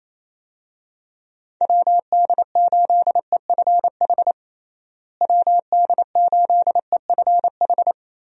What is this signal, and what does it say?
Morse code, heard as a keyed tone, about 27 wpm, 700 Hz. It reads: WD8EFH WD8EFH
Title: USA Call Signs - 6 Characters - Repeated 3x - 28wpm